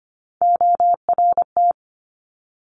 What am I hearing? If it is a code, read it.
Morse code, heard as a keyed tone, about 25 wpm, 700 Hz. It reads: ORT
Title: Top Three Letter Combinations in Words 25wpm